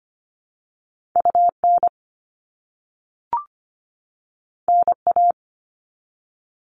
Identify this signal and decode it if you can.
Morse code, heard as a keyed tone, about 25 wpm, 700 Hz. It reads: UN NA
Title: All Cut Numbers - 2-Digit 25wpm